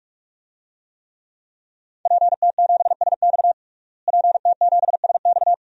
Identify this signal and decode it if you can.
Morse code, heard as a keyed tone, about 44 wpm, 700 Hz. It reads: PT7SX PT7SX
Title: DXCC-100 Call Signs - 5 Characters - Repeated 2x - 45wpm